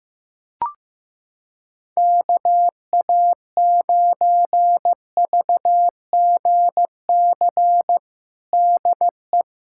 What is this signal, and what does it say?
Morse code, heard as a keyed tone, about 15 wpm, 700 Hz. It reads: KA9VGC DE
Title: USA Callsign DE Callsign - 6 Characters - 15wpm